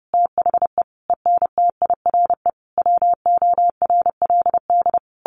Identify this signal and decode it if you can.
Morse code, heard as a keyed tone, about 30 wpm, 700 Hz. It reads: THE ENTIRE WORLD
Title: A Sound Of Thunder 30wpm